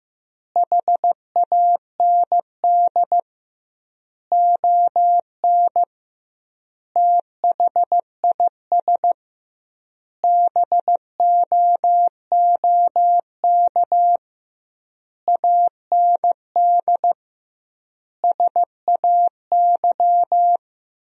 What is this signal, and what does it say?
Morse code, heard as a keyed tone, about 15 wpm, 700 Hz. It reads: HAND ON THIS BOOK AND SAY
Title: 2x Word-Spacing - Sentences from Top 400 Words - 15wpm